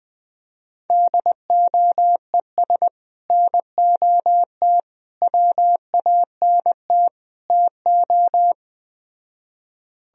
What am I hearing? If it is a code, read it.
Morse code, heard as a keyed tone, about 20 wpm, 700 Hz. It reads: DOES NOT WANT TO
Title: Speed-Racing - Sets of 4 Words 17wpm